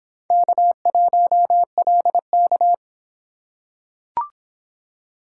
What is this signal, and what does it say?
Morse code, heard as a keyed tone, about 24 wpm, 700 Hz. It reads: K1LK